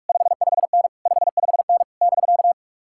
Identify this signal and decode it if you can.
Morse code, heard as a keyed tone, about 56 wpm, 700 Hz. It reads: 55N 55N <BK>
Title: Speed-Racing Common QSO Phrases Encore - 45wpm